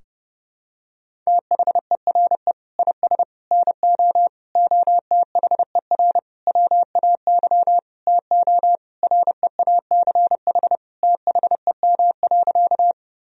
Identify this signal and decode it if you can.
Morse code, heard as a keyed tone, about 32 wpm, 700 Hz. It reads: THERE IS NO OTHER WAY TO REACH THEM.
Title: Sentences from Top 1000 Words - Review Part 2 - 30wpm